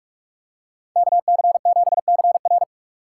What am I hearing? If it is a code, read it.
Morse code, heard as a keyed tone, about 45 wpm, 700 Hz. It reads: KC7CR